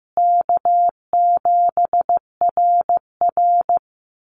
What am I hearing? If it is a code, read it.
Morse code, heard as a keyed tone, about 14 wpm, 700 Hz. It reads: K7RR
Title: Call Signs Repeated 2x - 15wpm